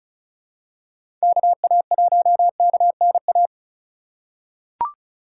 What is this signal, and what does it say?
Morse code, heard as a keyed tone, about 34 wpm, 700 Hz. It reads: KA1KNA
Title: USA Call Signs - 6 Characters - 2x Character Spacing - 35wpm